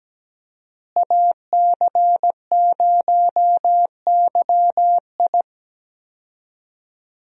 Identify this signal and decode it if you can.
Morse code, heard as a keyed tone, about 17 wpm, 700 Hz. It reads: AC0YI
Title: USA Call Signs - 5 Characters - Repeated 2x - 17wpm